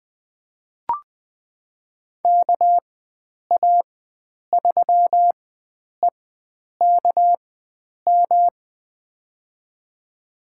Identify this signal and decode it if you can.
Morse code, heard as a keyed tone, about 20 wpm, spaced as for 9 wpm, 700 Hz. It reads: KA3EKM
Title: USA Call Signs - 6 Characters - 4x Character Spacing - 20wpm